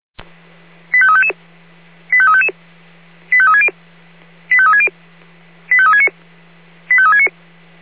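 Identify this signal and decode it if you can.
Five-tone selective call (ZVEI), heard as quick runs of five tones, tones of about 70 ms. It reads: 85355 85355 85360 85360 85369 85369
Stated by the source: ZVEI five-tone selective call received on SDR.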